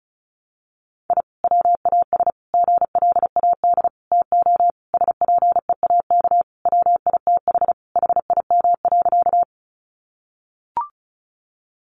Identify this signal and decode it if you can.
Morse code, heard as a keyed tone, about 34 wpm, 700 Hz. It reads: I WAS GLAD TO SPEAK WITH HIM.
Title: Sentences from Top 1100 Words 35wpm